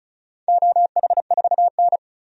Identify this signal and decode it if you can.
Morse code, heard as a keyed tone, about 36 wpm, 700 Hz. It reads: OH4N
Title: DXCC-100 Call Signs - 4 Characters - 35wpm